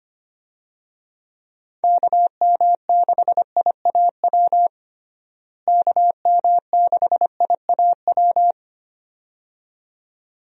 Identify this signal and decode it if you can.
Morse code, heard as a keyed tone, about 25 wpm, 700 Hz. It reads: KM6IAW KM6IAW
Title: USA Call Signs - 6 Characters - Repeated 2x - 25wpm